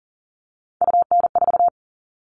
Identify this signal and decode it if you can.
Morse code, heard as a keyed tone, about 40 wpm, 700 Hz. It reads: UN4
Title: All Cut Numbers - 3-Digit 40wpm